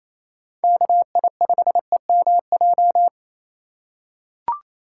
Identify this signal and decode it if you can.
Morse code, heard as a keyed tone, about 27 wpm, 700 Hz. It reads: KI5EMJ
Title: USA Call Signs - 6 Characters - 2x Character Spacing - 28wpm